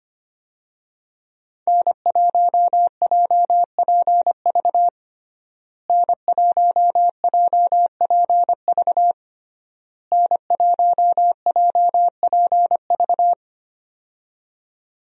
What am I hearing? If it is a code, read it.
Morse code, heard as a keyed tone, about 25 wpm, 700 Hz. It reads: N1JPV N1JPV N1JPV
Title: USA Call Signs - 5 Characters - Repeated 3x - 25wpm